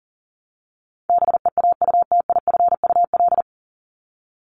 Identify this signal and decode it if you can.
Morse code, heard as a keyed tone, about 40 wpm, 700 Hz. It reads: BEAUTIFUL
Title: Top 751-1000 Words 40wpm